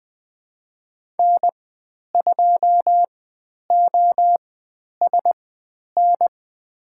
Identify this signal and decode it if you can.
Morse code, heard as a keyed tone, about 20 wpm, spaced as for 10 wpm, 700 Hz. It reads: N2OSN
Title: 095 Lesson 16g - Letter c - Call Signs 20wpm